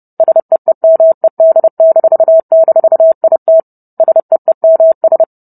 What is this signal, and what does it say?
Morse code, heard as a keyed tone, about 30 wpm, 650 Hz. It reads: SEEMED--IT SEEMS